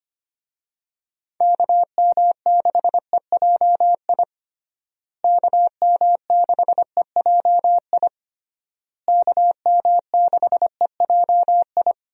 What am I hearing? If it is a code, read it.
Morse code, heard as a keyed tone, about 25 wpm, 700 Hz. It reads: KM6EJI KM6EJI KM6EJI
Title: USA Call Signs - 6 Characters - Repeated 3x - 25wpm